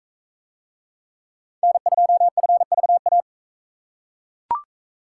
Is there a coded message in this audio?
Yes: Morse code from 1.6 to 3.2 s, heard as a keyed tone, about 42 wpm, 700 Hz. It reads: N2FVA